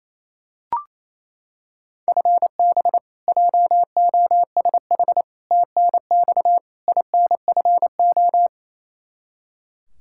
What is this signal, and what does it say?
Morse code, heard as a keyed tone, about 28 wpm, 700 Hz. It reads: FB JOSH TNX INFO
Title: Practice QSO - Exchange 3 - More Info - 28wpm